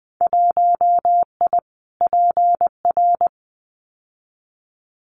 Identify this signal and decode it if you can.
Morse code, heard as a keyed tone, about 20 wpm, 700 Hz. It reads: 1I PR
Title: Winter Field Day Exchanges Encore 20wpm